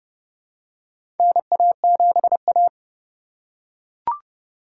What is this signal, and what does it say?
Morse code, heard as a keyed tone, about 28 wpm, 700 Hz. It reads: NA7A
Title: USA Call Signs - 4 Characters - 2x Character Spacing - 30wpm